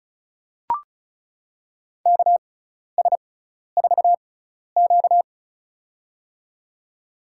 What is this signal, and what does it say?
Morse code, heard as a keyed tone, about 36 wpm, spaced as for 12 wpm, 700 Hz. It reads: KS4Q
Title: USA Call Signs - 6x Character Spacing - 35wpm